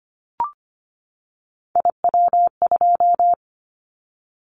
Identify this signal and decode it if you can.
Morse code, heard as a keyed tone, about 25 wpm, 700 Hz. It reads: IW2
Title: Callsign Prefixes for Top 50 DXCC - 25wpm